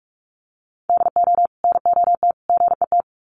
Morse code, heard as a keyed tone, about 44 wpm, 700 Hz. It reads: DO NOT GET
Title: Sets of 3 Words 45wpm